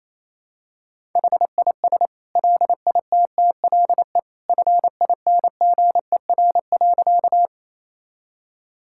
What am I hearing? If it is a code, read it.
Morse code, heard as a keyed tone, about 28 wpm, 700 Hz. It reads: HIS LITTLE FINGER.
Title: Sentences from Top 1600 Words 28wpm